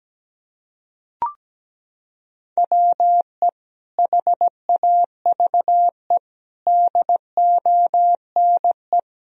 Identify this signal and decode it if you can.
Morse code, heard as a keyed tone, about 17 wpm, 700 Hz. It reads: WE HAVE DONE
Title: Sentences from Top 1000 Words - Review Part 1 - 17wpm